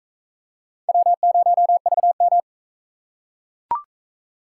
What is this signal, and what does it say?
Morse code, heard as a keyed tone, about 40 wpm, 700 Hz. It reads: W0VM